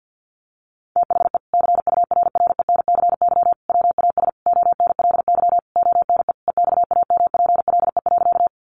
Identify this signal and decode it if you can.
Morse code, heard as a keyed tone, about 50 wpm, 700 Hz. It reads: THE CURRENCY WAS ONLY ONE EXAMPLE.
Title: Sentences from Top 1500 Words 50wpm